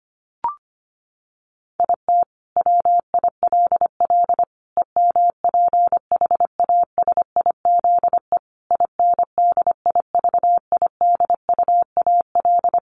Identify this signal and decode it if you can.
Morse code, heard as a keyed tone, about 25 wpm, 700 Hz. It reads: IT WILL EMPHASIZE INDIVIDUAL
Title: Sentences from Top 1500 Words 25wpm